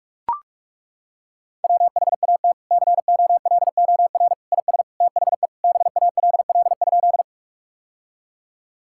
Morse code, heard as a keyed tone, about 41 wpm, 700 Hz. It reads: WHAT COLOR IS THE BALL?
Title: Sentences from Top 1000 Words - Review Part 6 - 45wpm